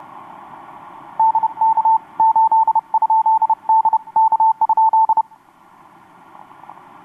Morse code, heard as a keyed tone, about 31 wpm, spaced as for 27 wpm, 890 Hz. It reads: DK8 ? DK?
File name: xw2cw rx6am etc 12m cw 26nov2012 1153